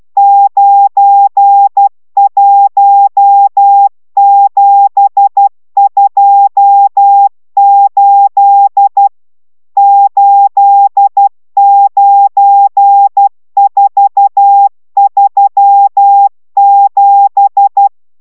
Morse code, heard as a keyed tone, about 12 wpm, 800 Hz. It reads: 91728 89437